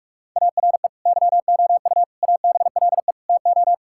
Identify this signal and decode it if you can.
Morse code, heard as a keyed tone, about 41 wpm, 700 Hz. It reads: ARE YOU ABLE TO
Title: Sentences from Top 1000 Words - Review Part 5 - 45wpm